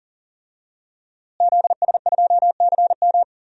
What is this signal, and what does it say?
Morse code, heard as a keyed tone, about 40 wpm, 700 Hz. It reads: ZS2CM
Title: DXCC-100 Call Signs - 5 Characters - Repeated 2x - 40wpm